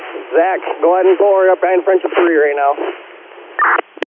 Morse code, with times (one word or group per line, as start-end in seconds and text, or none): none